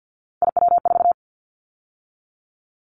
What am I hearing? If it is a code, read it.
Morse code, heard as a keyed tone, about 50 wpm, 700 Hz. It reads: IW4
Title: Callsign Prefixes for Top 50 DXCC - 50wpm